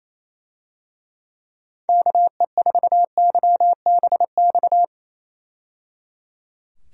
Morse code, heard as a keyed tone, about 28 wpm, 700 Hz. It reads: KE4YBX
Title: USA Call Signs - 6 Characters - 28wpm